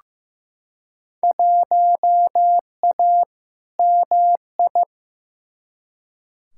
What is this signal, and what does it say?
Morse code, heard as a keyed tone, about 15 wpm, 700 Hz. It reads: 1A MI
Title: Field Day Exchanges - 15wpm